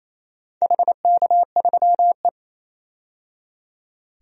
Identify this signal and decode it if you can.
Morse code, heard as a keyed tone, about 27 wpm, 700 Hz. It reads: HK3E
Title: DXCC-100 Call Signs - 4 Characters - 28wpm